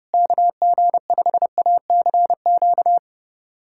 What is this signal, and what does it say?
Morse code, heard as a keyed tone, about 29 wpm, 700 Hz. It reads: KG5ACQ
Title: USA Callsign DE Callsign - 6 Characters - 30wpm